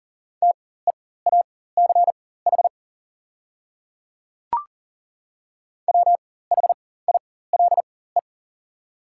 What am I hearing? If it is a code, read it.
Morse code, heard as a keyed tone, about 40 wpm, spaced as for 19 wpm, 700 Hz. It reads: TEACH WHILE
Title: 4x Character Spacing - Top 500 Words - 40wpm